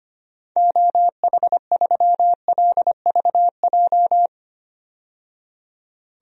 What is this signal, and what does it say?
Morse code, heard as a keyed tone, about 25 wpm, 700 Hz. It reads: OH3LVJ
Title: DXCC-100 Call Signs - 6 Characters - 25wpm